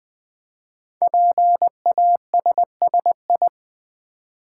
Morse code, heard as a keyed tone, about 20 wpm, 700 Hz. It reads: PASSI
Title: Build A Word - 5 Letters At A Time - 10-Letter Words - 20wpm